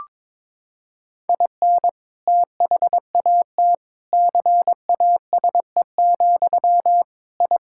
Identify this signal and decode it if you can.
Morse code, heard as a keyed tone, about 22 wpm, 700 Hz. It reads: IN THAT CASE, I
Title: Sentences from Top 1000 Words - Review Part 6 - 22wpm